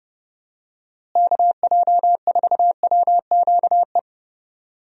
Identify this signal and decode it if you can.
Morse code, heard as a keyed tone, about 32 wpm, 700 Hz. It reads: KJ4WQE
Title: USA Call Signs - 6 Characters - Repeated 2x - 30wpm